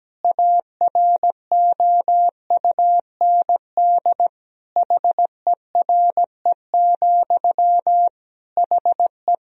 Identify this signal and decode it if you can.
Morse code, heard as a keyed tone, about 17 wpm, 700 Hz. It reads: AROUND HERE, HE
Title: Sentences from Top 1000 Words - Review Part 3 - 17wpm